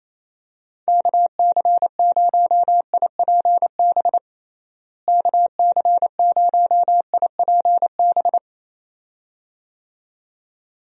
Morse code, heard as a keyed tone, about 28 wpm, 700 Hz. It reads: KC0IPB KC0IPB
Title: USA Call Signs - 6 Characters - Repeated 2x - 28wpm